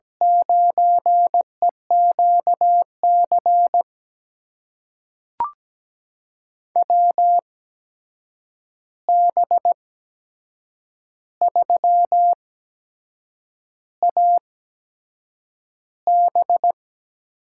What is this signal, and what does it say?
Morse code, heard as a keyed tone, about 17 wpm, spaced as for 7 wpm, 700 Hz. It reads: 9EQC WB3AB
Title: USA Call Signs - 6 Characters - 8x Character Spacing - 17wpm